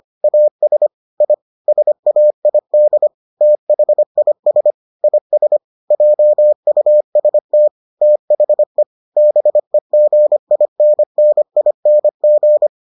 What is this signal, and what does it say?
Morse code, heard as a keyed tone, about 25 wpm, 600 Hz. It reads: AS I SAID THIS IS JUST THE BEGINNING